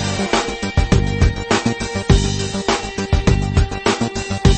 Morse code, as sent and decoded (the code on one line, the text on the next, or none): none